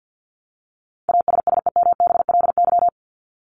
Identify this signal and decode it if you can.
Morse code, heard as a keyed tone, about 50 wpm, 700 Hz. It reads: ASSEMBLY
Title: LY Suffix - Top 100 Words with Base Words 50wpm